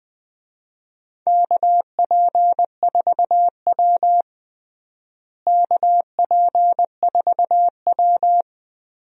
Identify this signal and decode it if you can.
Morse code, heard as a keyed tone, about 21 wpm, 700 Hz. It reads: KP4W KP4W
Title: DXCC-100 Call Signs - 4 Characters - Repeated 2x - 20wpm